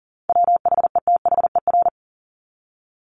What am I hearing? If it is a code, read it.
Morse code, heard as a keyed tone, about 40 wpm, 700 Hz. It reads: WHETHER
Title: Top 501-750 Words 40wpm